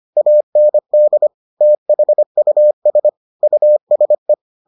Morse code, heard as a keyed tone, about 25 wpm, 600 Hz. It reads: AND THUS USE